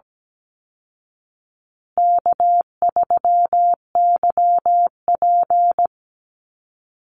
Morse code, heard as a keyed tone, about 17 wpm, 700 Hz. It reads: K3YP
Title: Call Signs Repeated 2x - 17wpm